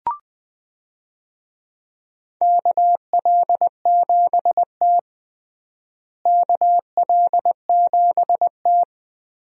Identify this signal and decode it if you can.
Morse code, heard as a keyed tone, about 21 wpm, 700 Hz. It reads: KL7T KL7T
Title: DXCC-100 Call Signs - 4 Characters - Repeated 2x - 20wpm